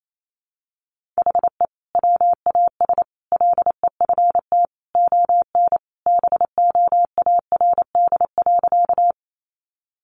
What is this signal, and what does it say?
Morse code, heard as a keyed tone, about 28 wpm, 700 Hz. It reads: HE WAS LEFT ON BOARD.
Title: Sentences from Top 700 Words 28wpm